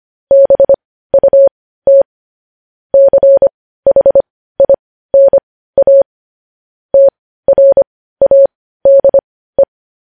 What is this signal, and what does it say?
Morse code, heard as a keyed tone, about 25 wpm, spaced as for 15 wpm, 550 Hz. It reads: BUT CHINA TRADE
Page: news-headlines-in-morse-code-at-15-wpm